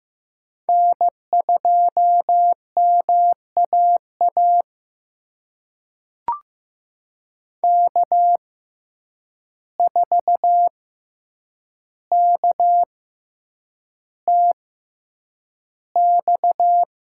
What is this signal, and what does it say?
Morse code, heard as a keyed tone, about 15 wpm, spaced as for 8 wpm, 700 Hz. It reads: N2MAA K4KTX
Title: USA Call Signs - 6x Character Spacing - 15wpm